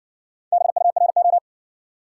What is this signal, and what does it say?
Morse code, heard as a keyed tone, about 60 wpm, 700 Hz. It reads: BURO